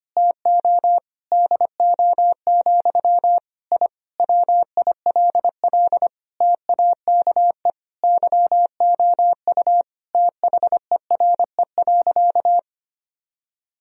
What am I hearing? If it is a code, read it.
Morse code, heard as a keyed tone, about 25 wpm, 700 Hz. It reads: TO DO, I WILL TAKE YOU THERE.
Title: Sentences from Top 1000 Words - Review Part 3 - 25wpm